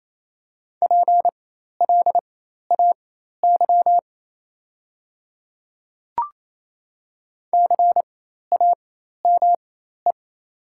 Morse code, heard as a keyed tone, about 28 wpm, spaced as for 13 wpm, 700 Hz. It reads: PLAY CAME
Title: 4x Character Spacing - Top 300 Words - 28wpm